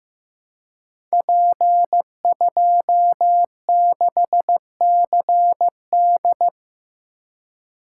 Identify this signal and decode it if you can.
Morse code, heard as a keyed tone, about 15 wpm, 700 Hz. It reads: P26CD
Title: Code Groups - 5 Letter:Numbers - 15wpm